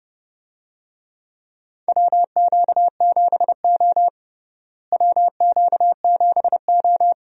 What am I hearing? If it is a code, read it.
Morse code, heard as a keyed tone, about 30 wpm, 700 Hz. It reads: WQ7O WQ7O
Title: USA Call Signs - 4 Characters - Repeated 2x - 30wpm